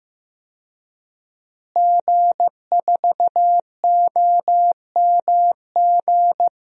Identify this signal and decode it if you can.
Morse code, heard as a keyed tone, about 15 wpm, 700 Hz. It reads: G4OMG
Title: DXCC-100 Call Signs - 5 Characters - Repeated 3x - 15wpm